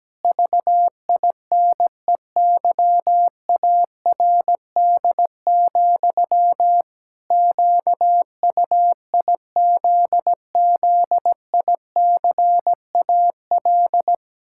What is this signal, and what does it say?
Morse code, heard as a keyed tone, about 17 wpm, 700 Hz. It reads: VINEYARD, QUIZZICAL
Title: Pangram Challenge v2 - 17wpm